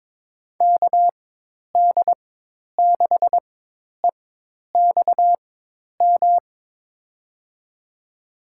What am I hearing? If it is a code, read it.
Morse code, heard as a keyed tone, about 22 wpm, spaced as for 10 wpm, 700 Hz. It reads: KD6EXM
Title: USA Call Signs - 6 Characters - 4x Character Spacing - 22wpm